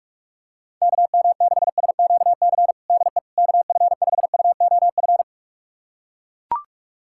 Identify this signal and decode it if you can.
Morse code, heard as a keyed tone, about 42 wpm, 700 Hz. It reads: KM6SQC DE KF5UOF
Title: USA Callsign DE Callsign - 6 Characters - 45wpm